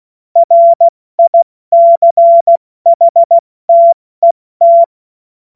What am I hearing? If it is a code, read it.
Morse code, heard as a keyed tone, about 16 wpm, 680 Hz. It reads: RICHTET
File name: Morsekurs / Witze / Witz 11. April 70 F80